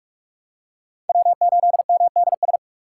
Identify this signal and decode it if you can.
Morse code, heard as a keyed tone, about 47 wpm, 690 Hz. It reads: W8MDS
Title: USA Call Signs - 5 Characters - 6x Character Spacing - 45wpm